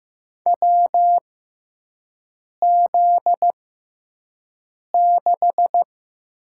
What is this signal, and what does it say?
Morse code, heard as a keyed tone, about 15 wpm, spaced as for 5 wpm, 700 Hz. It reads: WZ6
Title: USA Call Signs - 4 Characters - 6x Character Spacing - 15wpm